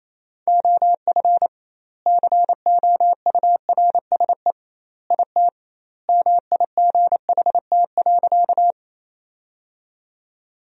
Morse code, heard as a keyed tone, about 28 wpm, 700 Hz. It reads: OF COURSE IT MIGHT.
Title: 2x Word-Spacing - Sentences from Top 500 Words - 28wpm